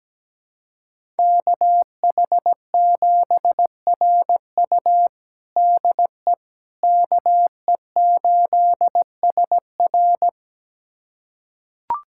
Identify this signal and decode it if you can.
Morse code, heard as a keyed tone, about 17 wpm, 700 Hz. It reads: KH7RU DE KE8SR
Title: USA Callsign DE Callsign - 5 Characters - 17wpm